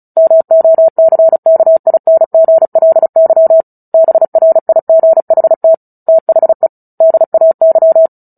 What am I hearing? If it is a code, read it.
Morse code, heard as a keyed tone, about 34 wpm, 650 Hz. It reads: MOCKINGLY BRIGHT THE DAY
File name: Book Two / 09-Chapter 9 - 35 WPM